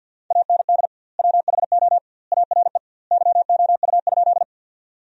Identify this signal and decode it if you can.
Morse code, heard as a keyed tone, about 48 wpm, 700 Hz. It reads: AND WHO ARE YOU?
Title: 2x Word-Spacing - Sentences from Top 100 Words - 50wpm